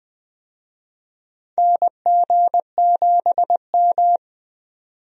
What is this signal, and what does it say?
Morse code, heard as a keyed tone, about 20 wpm, 700 Hz. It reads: NG7M
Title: USA Call Signs - 4 Characters - 20wpm